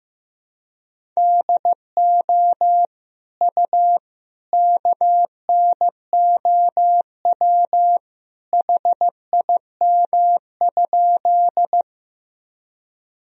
Morse code, heard as a keyed tone, about 15 wpm, 700 Hz. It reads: DO U KNOW HIM?
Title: Sentences from Top 100 Words and Abbreviations 15wpm